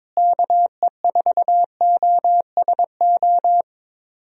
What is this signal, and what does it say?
Morse code, heard as a keyed tone, about 22 wpm, 700 Hz. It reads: KE4OSO
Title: USA Call Signs - 6 Characters - Repeated 3x - 22wpm